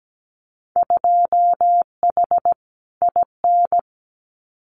Morse code, heard as a keyed tone, about 17 wpm, 700 Hz. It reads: 2H IN
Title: Winter Field Day Exchanges Encore 17wpm